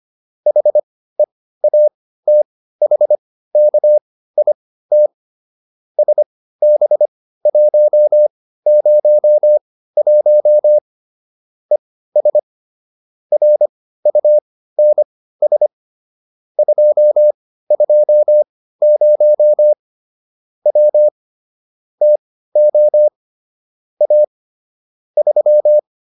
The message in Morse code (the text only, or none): HEATHKIT SB101 ES RUNS 220 W TO A 3